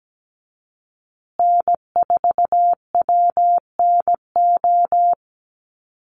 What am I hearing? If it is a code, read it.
Morse code, heard as a keyed tone, about 17 wpm, 700 Hz. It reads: N4WNO
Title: USA Call Signs Repeated 2x 17wpm